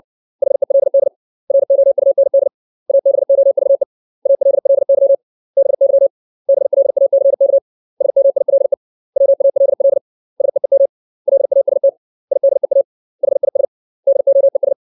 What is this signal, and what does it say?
Morse code, heard as a keyed tone, about 60 wpm, 550 Hz. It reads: HELD ROUND ABOVE ALLY BY BLACK SMILE WALL SEEM LAST IDEA HIS DOES
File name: 632 words / 632 Words 60WPM